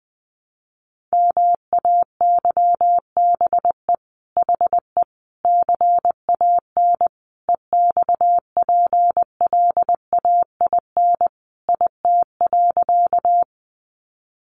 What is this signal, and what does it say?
Morse code, heard as a keyed tone, about 20 wpm, 700 Hz. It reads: MAYBE HE CAN EXPLAIN IT.
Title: Sentences from Top 1100 Words 20wpm